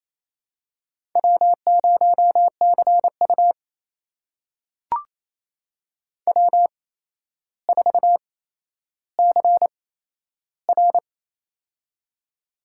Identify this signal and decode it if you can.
Morse code, heard as a keyed tone, about 28 wpm, spaced as for 12 wpm, 700 Hz. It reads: W0CU W4CR
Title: USA Call Signs - 4 Characters - 8x Character Spacing - 28wpm